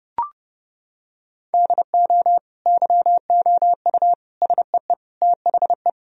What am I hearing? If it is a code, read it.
Morse code, heard as a keyed tone, about 30 wpm, 700 Hz. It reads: DO YOU SEE THE
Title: Sentences from Top 1000 Words - Review Part 4 - 30wpm